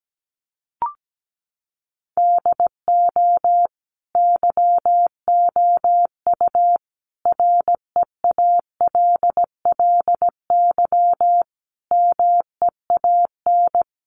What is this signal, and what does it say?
Morse code, heard as a keyed tone, about 17 wpm, 700 Hz. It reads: DO YOU REALLY MEAN
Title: Sentences from Top 600 Words 17wpm